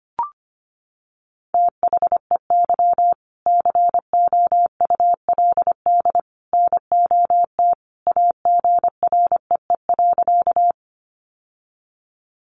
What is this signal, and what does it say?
Morse code, heard as a keyed tone, about 25 wpm, 700 Hz. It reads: THEY COULD NOT AGREE.
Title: Sentences from Top 1100 Words 25wpm